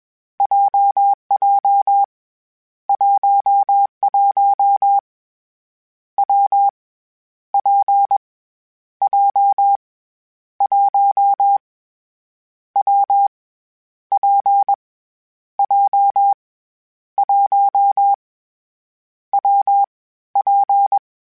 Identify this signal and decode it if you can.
Morse code, heard as a keyed tone, about 21 wpm, 800 Hz. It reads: JJ 11 W P J 1 W P J 1 W P
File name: Variable Speed / Variable Speed-mp3s / WPJ WPJ1 sp @20